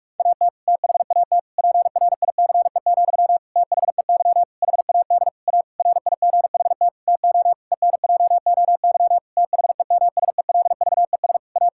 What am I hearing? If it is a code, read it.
Morse code, heard as a keyed tone, about 46 wpm, 700 Hz. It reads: AT THAT PRICE, THEY HAD A RIGHT TO ENJOY THEMSELVES A
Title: Sentences from Top 1000 Words - Review Part 5 - 45wpm